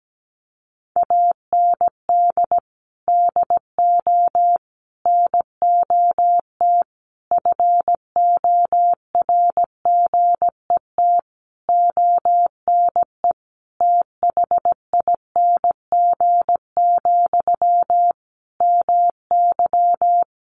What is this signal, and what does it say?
Morse code, heard as a keyed tone, about 17 wpm, 700 Hz. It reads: AND DO NOT FORGET ONE THING, MY
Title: Sentences from Top 1200 Words 17wpm